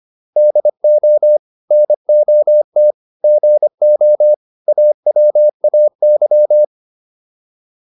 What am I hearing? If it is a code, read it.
Morse code, heard as a keyed tone, about 25 wpm, 600 Hz. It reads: DO NOT GO AWAY
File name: WD304 25